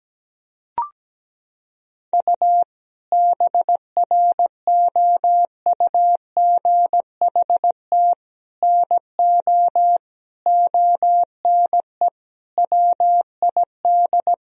Morse code, heard as a keyed tone, about 17 wpm, 700 Hz. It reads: U BROUGHT NO ONE WID
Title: Sentences from Top 500 Words and Abbreviations 17wpm